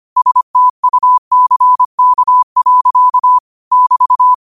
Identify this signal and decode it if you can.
Morse code, heard as a keyed tone, about 25 wpm, 1000 Hz. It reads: STUCK. <BT>
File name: qotdfiles / 25wpm